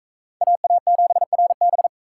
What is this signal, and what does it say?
Morse code, heard as a keyed tone, about 42 wpm, 700 Hz. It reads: AA7RB